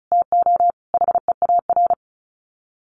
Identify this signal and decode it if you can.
Morse code, heard as a keyed tone, about 34 wpm, 700 Hz. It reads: TO HEAR
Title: Sets of 5 Words 35wpm